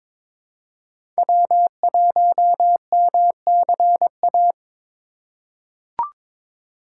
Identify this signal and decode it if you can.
Morse code, heard as a keyed tone, about 22 wpm, 700 Hz. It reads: W1MCA
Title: USA Call Signs - 5 Characters - 4x Character Spacing - 22wpm